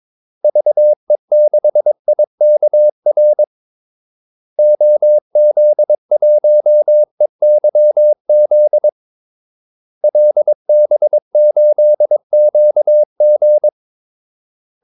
Morse code, heard as a keyed tone, about 22 wpm, 600 Hz. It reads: VE6IKR OZ1EYZ LB8QG
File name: Callsigns-22wpm